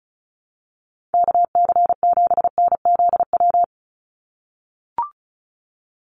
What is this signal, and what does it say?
Morse code, heard as a keyed tone, about 34 wpm, 700 Hz. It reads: KC7NZW
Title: USA Call Signs 35wpm